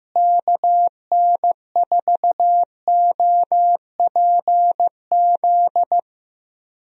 Morse code, heard as a keyed tone, about 15 wpm, 700 Hz. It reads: KN4OPZ
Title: USA Call Signs - 6 Characters - Repeated 2x - 15wpm